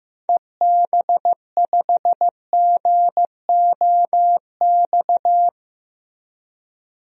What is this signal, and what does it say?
Morse code, heard as a keyed tone, about 15 wpm, 700 Hz. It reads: EB5GOX
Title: DXCC-100 Call Signs - 6 Characters - Repeated 3x - 15wpm